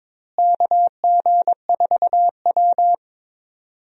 Morse code, heard as a keyed tone, about 22 wpm, 700 Hz. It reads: KG4W
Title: USA Call Signs - 4 Characters - Repeated 3x - 22wpm